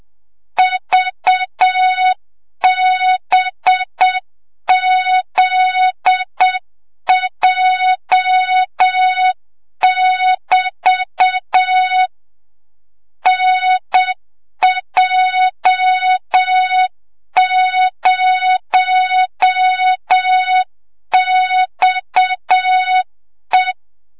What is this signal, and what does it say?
Morse code, heard as a keyed tone, about 7 wpm, 3700 Hz, with noise. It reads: VBZJ<BT> NJ0XE
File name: Morze-HA2MN / hang / 25 GYAKORLAT